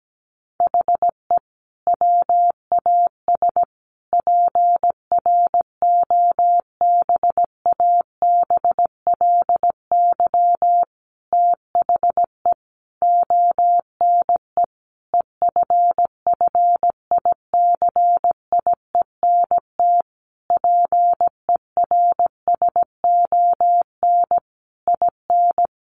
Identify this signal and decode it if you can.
Morse code, heard as a keyed tone, about 17 wpm, 700 Hz. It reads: HE WAS PROBABLY THE ONE EFFICIENT PERSON IN
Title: Sentences from Top 1300 Words 17wpm